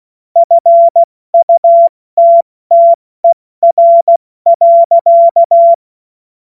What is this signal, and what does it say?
Morse code, heard as a keyed tone, about 16 wpm, 680 Hz. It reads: FUTTER.
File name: Witz-15-Juli-2025-70-F80